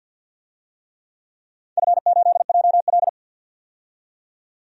Morse code, heard as a keyed tone, about 46 wpm, 700 Hz. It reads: F8JL
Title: DXCC-100 Call Signs - 4 Characters - 50wpm